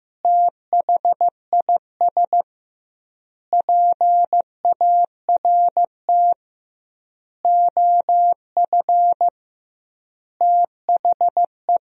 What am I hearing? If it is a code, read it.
Morse code, heard as a keyed tone, about 15 wpm, 700 Hz. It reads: THIS PART OF THE
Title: 2x Word-Spacing - Sentences from Top 300 Words - 15wpm